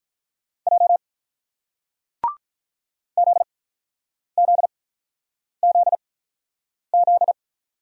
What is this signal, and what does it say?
Morse code, heard as a keyed tone, about 48 wpm, spaced as for 9 wpm, 700 Hz. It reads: J ZZZZ